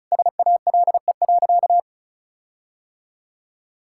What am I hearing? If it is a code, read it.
Morse code, heard as a keyed tone, about 36 wpm, 700 Hz. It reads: SALE.
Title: Sentences from Top 1600 Words 35wpm